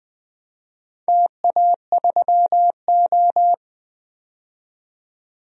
Morse code, heard as a keyed tone, about 20 wpm, 700 Hz. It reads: TA3O
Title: DXCC-100 Call Signs - 4 Characters - 20wpm